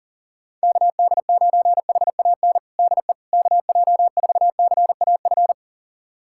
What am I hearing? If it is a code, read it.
Morse code, heard as a keyed tone, about 37 wpm, 700 Hz. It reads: KD9HAN DE KJ4CAF
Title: USA Callsign DE Callsign 40wpm